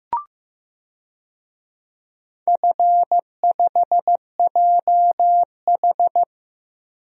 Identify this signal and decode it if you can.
Morse code, heard as a keyed tone, about 15 wpm, 700 Hz. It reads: F5JH
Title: DXCC-100 Call Signs - 4 Characters - Repeated 2x - 15wpm